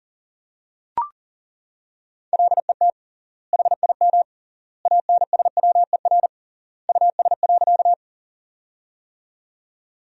Morse code, heard as a keyed tone, about 40 wpm, 700 Hz. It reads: LET HIM ANSWER US.
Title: 3x Word-Spacing - Sentences from Top 300 Words - 40wpm